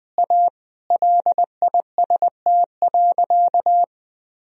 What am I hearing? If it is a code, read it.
Morse code, heard as a keyed tone, about 20 wpm, 700 Hz. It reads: A LIST.
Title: Sentences from Top 1000 Words - Review Part 2 - 20wpm